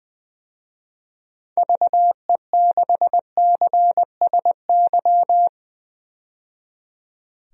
Morse code, heard as a keyed tone, about 20 wpm, 700 Hz. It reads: VE6CSY
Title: DXCC-100 Call Signs - 6 Characters - 20wpm